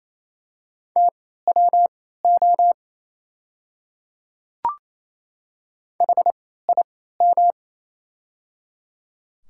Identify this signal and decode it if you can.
Morse code, heard as a keyed tone, about 28 wpm, spaced as for 16 wpm, 700 Hz. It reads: TWO HIM
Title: 3x Character Spacing - Top 100 Words - 28wpm